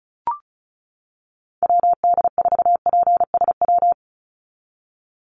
Morse code, heard as a keyed tone, about 34 wpm, 700 Hz. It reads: WD4PSW
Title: USA Call Signs 35wpm